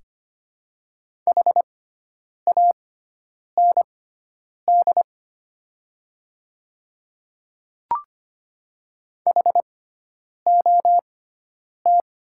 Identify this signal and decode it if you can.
Morse code, heard as a keyed tone, about 25 wpm, spaced as for 9 wpm, 700 Hz. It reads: HAND HOT
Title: 6x Character Spacing - Top 300 Words - 25wpm